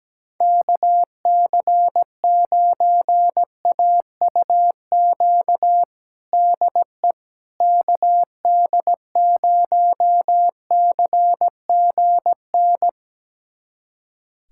Morse code, heard as a keyed tone, about 17 wpm, 700 Hz. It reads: KC9AUQ DE KD0CGN
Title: USA Callsign DE Callsign - 6 Characters - 17wpm